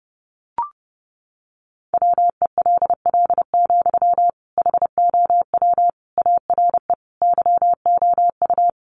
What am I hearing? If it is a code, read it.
Morse code, heard as a keyed tone, about 28 wpm, 700 Hz. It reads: WELL, HOW ARE YOU
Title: Sentences from Top 1200 Words 30wpm